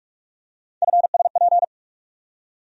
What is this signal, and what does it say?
Morse code, heard as a keyed tone, about 46 wpm, 700 Hz. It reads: FSP
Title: USA Call Sign Postfixes - 3 Letters 45wpm